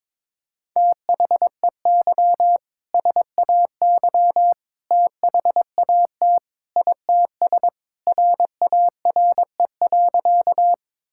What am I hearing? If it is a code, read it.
Morse code, heard as a keyed tone, about 22 wpm, 700 Hz. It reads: THEY SAY THAT ITS RARE.
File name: practice / 22wpm / Sentences with 2-Word Phrases - 22wpm